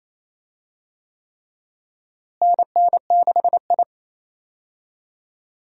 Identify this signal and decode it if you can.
Morse code, heard as a keyed tone, about 28 wpm, 700 Hz. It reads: NN6I
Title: USA Call Signs - 4 Characters - Repeated 3x - 28wpm